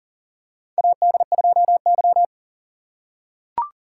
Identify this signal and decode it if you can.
Morse code, heard as a keyed tone, about 39 wpm, 700 Hz. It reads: AD2Y
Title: USA Call Signs - 4x Character Spacing - 40wpm